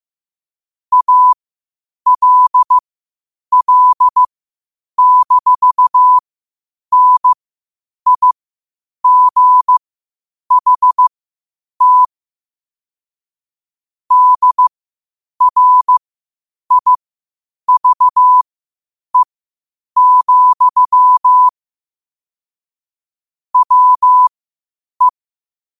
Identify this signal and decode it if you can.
Morse code, heard as a keyed tone, about 15 wpm, spaced as for 8 wpm, 1000 Hz. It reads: ALL-NIGHT DRIVE, WE
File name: qotdfiles / 8wpm